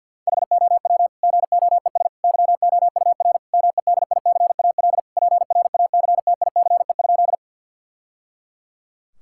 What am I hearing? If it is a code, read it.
Morse code, heard as a keyed tone, about 50 wpm, 700 Hz. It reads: HOW GOES YOUR MEDICAL PRACTICE?
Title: Sentences from Top 1000 Words - Review Part 5 - 50wpm